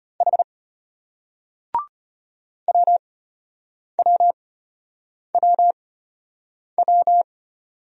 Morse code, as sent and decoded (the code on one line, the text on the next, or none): .... / .-- .-- .-- .--
H WWWW